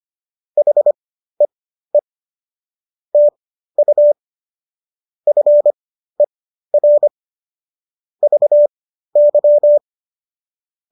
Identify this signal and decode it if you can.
Morse code, heard as a keyed tone, about 25 wpm, spaced as for 13 wpm, 590 Hz. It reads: HEE TU FER VY